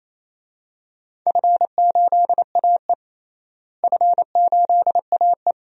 Morse code, heard as a keyed tone, about 28 wpm, 700 Hz. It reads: F8AE F8AE
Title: DXCC-100 Call Signs - 4 Characters - Repeated 2x - 28wpm